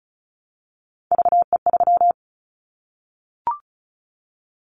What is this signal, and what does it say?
Morse code, heard as a keyed tone, about 36 wpm, 700 Hz. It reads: VE3
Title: Callsign Prefixes for Top 50 DXCC - 35wpm